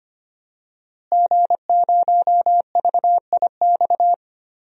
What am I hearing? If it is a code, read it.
Morse code, heard as a keyed tone, about 25 wpm, 700 Hz. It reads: G0VIX
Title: DXCC-100 Call Signs - 5 Characters - Repeated 3x - 25wpm